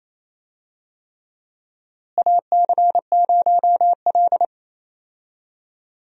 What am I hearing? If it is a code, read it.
Morse code, heard as a keyed tone, about 28 wpm, 700 Hz. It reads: AC0L
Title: USA Call Signs - 4 Characters - Repeated 3x - 28wpm